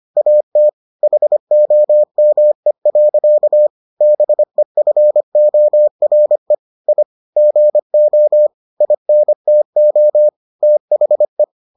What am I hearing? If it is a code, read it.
Morse code, heard as a keyed tone, about 25 wpm, 600 Hz. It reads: AT HOME. BEFORE I GO INTO THE